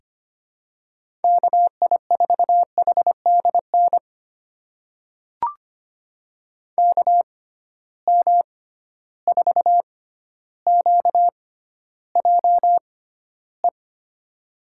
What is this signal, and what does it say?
Morse code, heard as a keyed tone, about 25 wpm, spaced as for 13 wpm, 700 Hz. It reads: KI4HDN KM4QJE
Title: USA Call Signs - 6 Characters - 6x Character Spacing - 25wpm